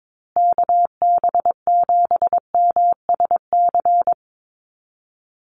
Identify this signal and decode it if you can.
Morse code, heard as a keyed tone, about 22 wpm, 700 Hz. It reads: KB7MSC
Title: USA Call Signs Repeated 2x 22wpm